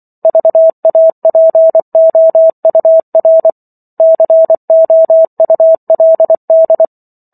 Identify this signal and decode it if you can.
Morse code, heard as a keyed tone, about 24 wpm, 650 Hz. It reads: VAPOUR COULD